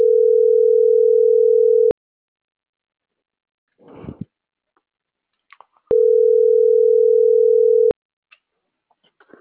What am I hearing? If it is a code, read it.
Ringback tone (North America), 440 + 480 Hz.
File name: CENTRAL MAERSK / out luis.herrera - .29207